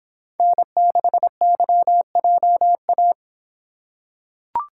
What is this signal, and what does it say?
Morse code, heard as a keyed tone, about 26 wpm, 700 Hz. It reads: N6YJA